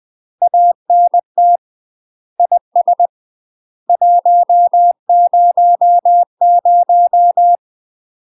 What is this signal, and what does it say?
Morse code, heard as a keyed tone, about 20 wpm, 700 Hz. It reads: ANT IS 100
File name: UK MASTER QSOs TRAINING / 20WPM / UK MASTER QSOs 0025